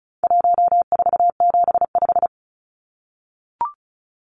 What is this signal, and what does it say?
Morse code, heard as a keyed tone, about 34 wpm, 700 Hz. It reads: 1475
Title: Numbers - 4-Digit 35wpm